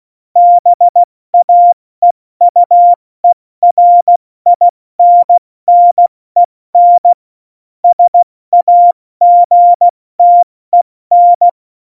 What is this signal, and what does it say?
Morse code, heard as a keyed tone, about 16 wpm, 700 Hz. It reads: BAEUERINNEN SAGTEN
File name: Morsekurs / Witze / Witz 21 Juli 2020, 70 F80